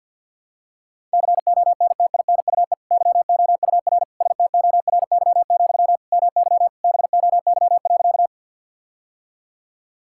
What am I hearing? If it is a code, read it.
Morse code, heard as a keyed tone, about 51 wpm, 700 Hz. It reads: CONTINUE YOUR STORY, MY BOY.
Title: Sentences from Top 1000 Words - Review Part 6 - 50wpm